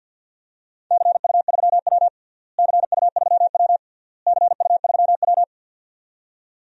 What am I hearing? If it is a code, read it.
Morse code, heard as a keyed tone, about 52 wpm, 700 Hz. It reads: CU3W CU3W CU3W
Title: DXCC-100 Call Signs - 4 Characters - Repeated 3x - 50wpm